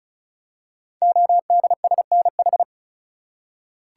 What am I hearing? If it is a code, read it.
Morse code, heard as a keyed tone, about 36 wpm, 700 Hz. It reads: ODSNH
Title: Code Groups - 5 Letter:Numbers - 35wpm